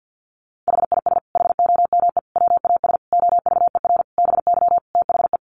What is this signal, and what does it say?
Morse code, heard as a keyed tone, about 50 wpm, 700 Hz. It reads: HIS HOME WAS OVER BY THE
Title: Sentences from Top 1500 Words 50wpm